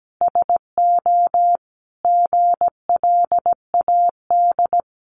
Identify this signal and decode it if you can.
Morse code, heard as a keyed tone, about 17 wpm, 700 Hz. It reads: SO GLAD
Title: Sentences from Top 800 Words 17wpm